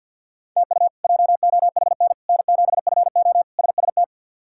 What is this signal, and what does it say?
Morse code, heard as a keyed tone, about 48 wpm, 700 Hz. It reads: TU JOHN N7FO SST
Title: K1USN SST QSO Practice - 50wpm